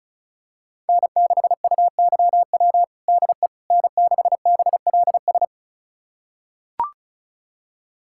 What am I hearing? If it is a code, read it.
Morse code, heard as a keyed tone, about 34 wpm, 700 Hz. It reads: N6UYW DE N6BLS
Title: USA Callsign DE Callsign - 5 Characters - 35wpm